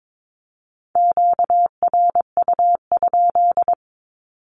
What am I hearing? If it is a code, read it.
Morse code, heard as a keyed tone, about 22 wpm, 700 Hz. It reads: QRU?
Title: Q-Codes 22wpm